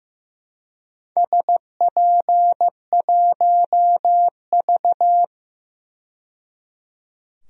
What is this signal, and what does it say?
Morse code, heard as a keyed tone, about 15 wpm, 700 Hz. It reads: SP1V
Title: DXCC-100 Call Signs - 4 Characters - 15wpm